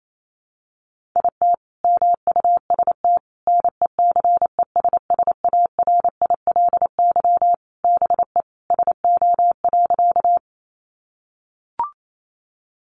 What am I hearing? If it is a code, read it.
Morse code, heard as a keyed tone, about 28 wpm, 700 Hz. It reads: IT MUST NECESSARILY BE SO.
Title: Sentences from Top 1400 Words 28wpm